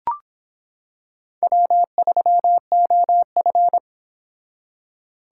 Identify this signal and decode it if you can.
Morse code, heard as a keyed tone, about 26 wpm, 700 Hz. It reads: W3OF